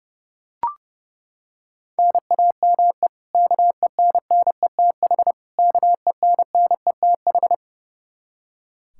Morse code, heard as a keyed tone, about 30 wpm, 700 Hz. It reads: NAME KENNETH KENNETH
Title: Practice QSO - Exchange 1 - Beginning - 30wpm